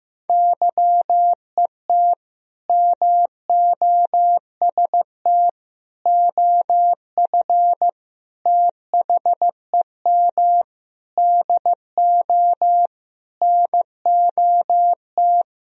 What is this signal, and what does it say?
Morse code, heard as a keyed tone, about 15 wpm, 700 Hz. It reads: YET MOST OF THEM DO NOT
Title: Sentences from Top 1000 Words - Review Part 1 - 15wpm